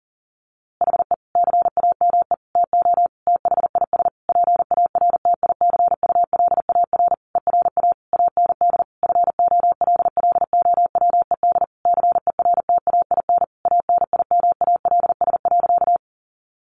Morse code, heard as a keyed tone, about 40 wpm, 700 Hz. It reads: HE CAME TO THIS PARTICULAR ERA AND FOLLOWED CERTAIN ANIMALS.
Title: A Sound Of Thunder 40wpm